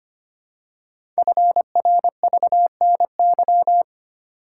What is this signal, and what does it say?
Morse code, heard as a keyed tone, about 25 wpm, 700 Hz. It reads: FRVNY
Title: Code Groups - 5 Letters - 25wpm